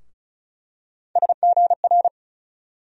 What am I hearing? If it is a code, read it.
Morse code, heard as a keyed tone, about 34 wpm, 700 Hz. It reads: SGR
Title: USA Call Sign Postfixes - 3 Letters 35wpm